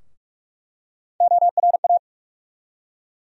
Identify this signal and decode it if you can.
Morse code, heard as a keyed tone, about 47 wpm, 700 Hz. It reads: ORA
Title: USA Call Sign Postfixes - 3 Letters 45wpm